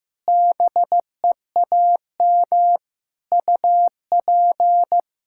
Morse code, heard as a keyed tone, about 15 wpm, 700 Hz. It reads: BEAM UP
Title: Practice QSO - Exchange 2 - Info - 15wpm